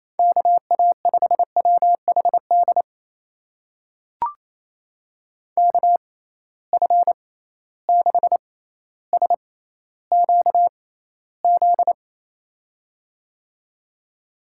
Morse code, heard as a keyed tone, about 28 wpm, spaced as for 14 wpm, 700 Hz. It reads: KA5WHD KF6SQZ
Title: USA Call Signs - 6 Characters - 6x Character Spacing - 28wpm